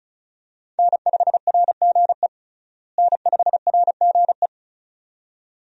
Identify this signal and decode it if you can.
Morse code, heard as a keyed tone, about 36 wpm, 700 Hz. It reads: N5RGE N5RGE
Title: USA Call Signs - 5 Characters - Repeated 2x - 35wpm